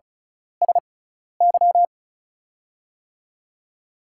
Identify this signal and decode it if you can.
Morse code, heard as a keyed tone, about 36 wpm, spaced as for 12 wpm, 700 Hz. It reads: SY
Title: USA Call Signs - 6 Characters - 6x Character Spacing - 35wpm